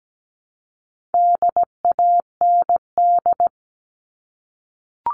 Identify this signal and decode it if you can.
Morse code, heard as a keyed tone, about 17 wpm, 700 Hz. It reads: DAND
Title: All Cut Numbers - 4-Digit 17wpm